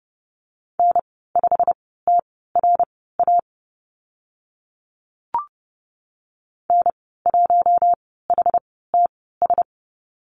Morse code, heard as a keyed tone, about 30 wpm, spaced as for 17 wpm, 700 Hz. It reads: N5TRA N1HTS
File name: Lesson 15g - Number 5 - Call Signs 30wpm